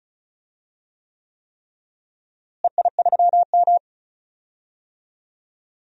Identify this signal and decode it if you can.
Morse code, heard as a keyed tone, about 34 wpm, 700 Hz. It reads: EI3M
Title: DXCC-100 Call Signs - 4 Characters - Repeated 3x - 35wpm